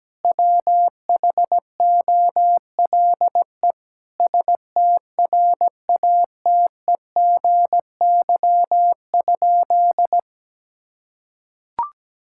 Morse code, heard as a keyed tone, about 17 wpm, 700 Hz. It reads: WHOLE STRATEGY?
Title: Sentences from Top 1000 Words - Review Part 5 - 17wpm